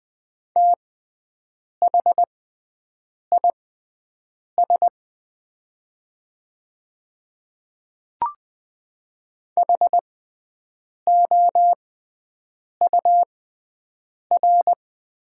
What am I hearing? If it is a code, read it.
Morse code, heard as a keyed tone, about 20 wpm, spaced as for 7 wpm, 700 Hz. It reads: THIS HOUR
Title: 6x Character Spacing - Top 500 Words - 20wpm